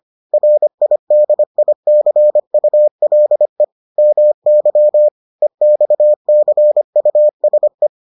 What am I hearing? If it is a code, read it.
Morse code, heard as a keyed tone, about 25 wpm, 600 Hz. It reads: RIDICULE MY EXCUSE